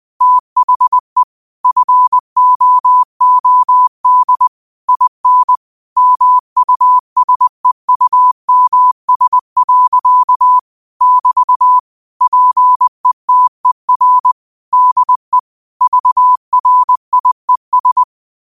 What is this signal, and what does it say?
Morse code, heard as a keyed tone, about 20 wpm, 1000 Hz. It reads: THE FOOD IN MUSEUMS. <BT> PETER DE VRIES
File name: qotdfiles / 20wpm